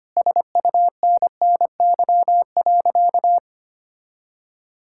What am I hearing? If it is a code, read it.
Morse code, heard as a keyed tone, about 25 wpm, 700 Hz. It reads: SUNNY.
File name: practice / 25wpm / Sentences with 3-Word Phrases - 25wpm